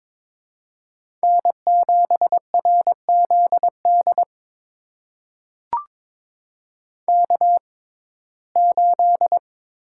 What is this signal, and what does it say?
Morse code, heard as a keyed tone, about 22 wpm, spaced as for 16 wpm, 700 Hz. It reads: N7RZD K8
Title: USA Call Signs - 6x Character Spacing - 22wpm